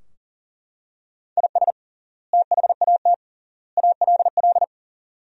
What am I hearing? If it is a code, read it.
Morse code, heard as a keyed tone, about 40 wpm, 700 Hz. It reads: IS THAT ALL
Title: 3x Word-Spacing - Sentences from Top 200 Words - 40wpm